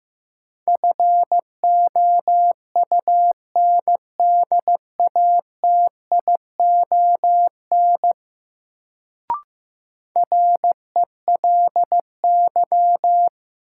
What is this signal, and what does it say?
Morse code, heard as a keyed tone, about 15 wpm, 700 Hz. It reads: FOUNDATION RELY
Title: Sentences from Top 1200 Words - New Word Drill Alt - 15wpm